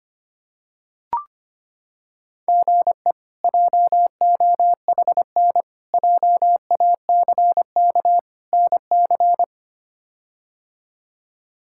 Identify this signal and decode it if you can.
Morse code, heard as a keyed tone, about 25 wpm, 700 Hz. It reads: GE JOHN JACK NC
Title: K1USN SST QSO Practice - 25wpm